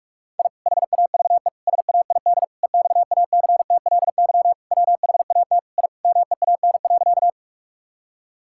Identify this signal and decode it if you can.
Morse code, heard as a keyed tone, about 46 wpm, 700 Hz. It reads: I HAVE SAID EXACTLY WHAT I MEAN.
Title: Sentences from Top 1000 Words - Review Part 6 - 45wpm